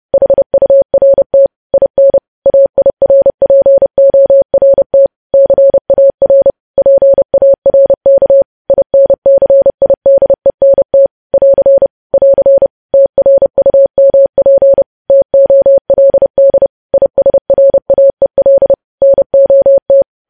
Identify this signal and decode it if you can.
Morse code, heard as a keyed tone, about 30 wpm, 550 Hz. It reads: HURT IN AIRPORT CAR PARK INCIDENT <AR> <AR> TRUMP TOLD ISRAEL NOT
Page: news headlines in morse code at 30 wpm